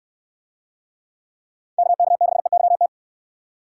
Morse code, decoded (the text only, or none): KK6EYT